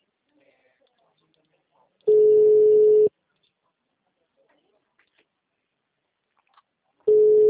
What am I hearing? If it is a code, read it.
Ringback tone (Europe, 425 Hz).